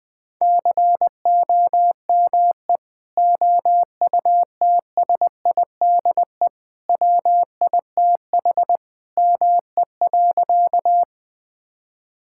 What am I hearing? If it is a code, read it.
Morse code, heard as a keyed tone, about 21 wpm, 700 Hz. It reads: COME OUTSIDE WITH ME.
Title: Sentences from Top 1000 Words - Review Part 6 - 20wpm